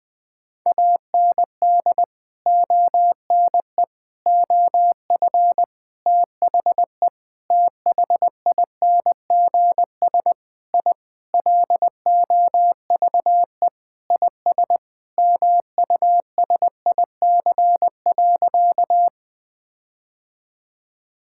Morse code, heard as a keyed tone, about 20 wpm, 700 Hz. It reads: AND ONE OF THE THINGS I LOVE IS MUSIC.
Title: Sentences with 5-Word Phrases - 20wpm